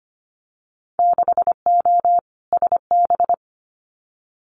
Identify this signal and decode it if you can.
Morse code, heard as a keyed tone, about 25 wpm, 700 Hz. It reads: 6O SB
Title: Winter Field Day Exchanges Encore 25wpm